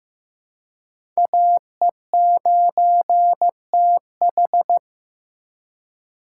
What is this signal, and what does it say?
Morse code, heard as a keyed tone, about 15 wpm, 700 Hz. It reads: AE9TH
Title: USA Call Signs - 5 Characters - 4x Character Spacing - 15wpm